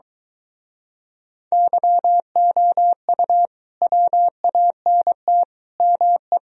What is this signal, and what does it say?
Morse code, heard as a keyed tone, about 23 wpm, 700 Hz. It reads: YOU WANT ME